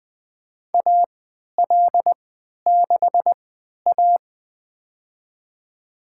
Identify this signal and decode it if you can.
Morse code, heard as a keyed tone, about 20 wpm, spaced as for 11 wpm, 700 Hz. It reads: AL6A
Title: USA Call Signs - 4 Characters - 3x Character Spacing - 20wpm